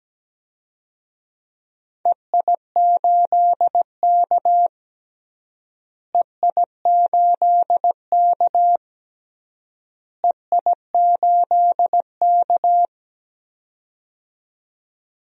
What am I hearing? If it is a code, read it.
Morse code, heard as a keyed tone, about 17 wpm, 700 Hz. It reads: EI8K EI8K EI8K
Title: DXCC-100 Call Signs - 4 Characters - Repeated 3x - 17wpm